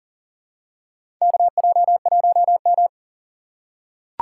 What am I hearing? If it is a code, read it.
Morse code, heard as a keyed tone, about 38 wpm, 700 Hz. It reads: KJ1M
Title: USA Call Signs - 4 Characters - 2x Character Spacing - 40wpm